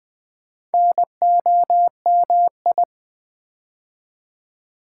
Morse code, heard as a keyed tone, about 20 wpm, 700 Hz. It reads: NOMI
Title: Build A Word - 4 Letters At A Time - 10-Letter Words - 20wpm